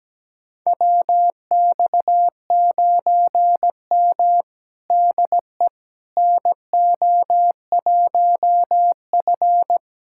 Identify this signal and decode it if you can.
Morse code, heard as a keyed tone, about 17 wpm, 700 Hz. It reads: WX9M DE NO1F
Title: USA Callsign DE Callsign - 4 Characters - 17wpm